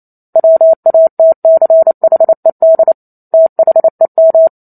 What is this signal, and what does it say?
Morse code, heard as a keyed tone, about 28 wpm, 650 Hz. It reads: WATCHED THEM
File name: Chapter 2 - 28 WPM